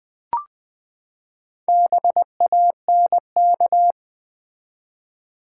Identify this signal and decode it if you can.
Morse code, heard as a keyed tone, about 20 wpm, 700 Hz. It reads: BANK
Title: Four Letter Words - 20wpm